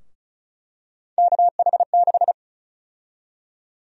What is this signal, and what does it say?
Morse code, heard as a keyed tone, about 36 wpm, 700 Hz. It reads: KH6
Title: USA Call Sign Prefixes - 35wpm